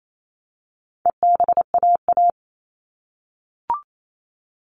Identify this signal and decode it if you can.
Morse code, heard as a keyed tone, about 28 wpm, 700 Hz. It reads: EBAA
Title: All Cut Numbers - 4-Digit 28wpm